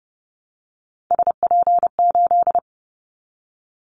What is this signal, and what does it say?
Morse code, heard as a keyed tone, about 30 wpm, 700 Hz. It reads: SP8
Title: Callsign Prefixes for Top 25 DXCC - 30wpm